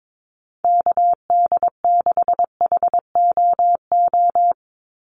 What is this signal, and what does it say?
Morse code, heard as a keyed tone, about 22 wpm, 700 Hz. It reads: KD6HOO
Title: USA Call Signs 22wpm